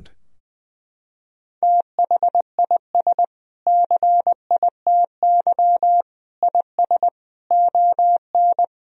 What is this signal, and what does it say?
Morse code, heard as a keyed tone, about 19 wpm, 700 Hz. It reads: THIS CITY IS ON
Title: Sentences from Top 1000 Words - Review Part 4 - 20wpm